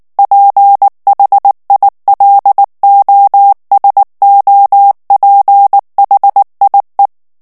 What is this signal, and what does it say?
Morse code, heard as a keyed tone, about 19 wpm, 800 Hz. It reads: PHILOSOPHIE
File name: CW / FICHIERS-MP3 / FRANCAIS TEXTE CLAIR 2 VITESSE 18 MOTS MINUTES